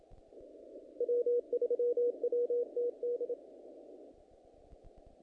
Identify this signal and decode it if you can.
Morse code, heard as a keyed tone, about 28 wpm, 480 Hz. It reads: W3WTD
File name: contests / naqp-cw-k5tr-op-k5pi- -2100